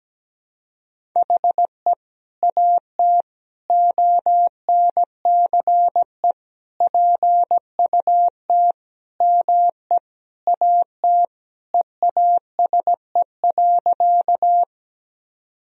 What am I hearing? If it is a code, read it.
Morse code, heard as a keyed tone, about 17 wpm, 700 Hz. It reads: HE AT ONCE PUT ME AT EASE.
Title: Sentences from Top 1000 Words - Review Part 1 - 17wpm